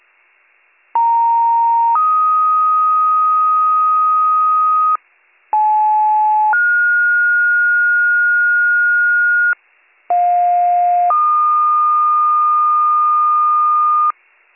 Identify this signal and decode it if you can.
Two-tone page, heard as a one-second tone, then a three-second tone, about 930 then 1250 Hz, about 860 then 1430 Hz, about 710 then 1220 Hz. It is heard from 0.9 to 14.1 s.